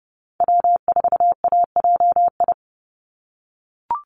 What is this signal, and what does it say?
Morse code, heard as a keyed tone, about 30 wpm, 700 Hz. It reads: W4AJI
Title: USA Call Signs 30wpm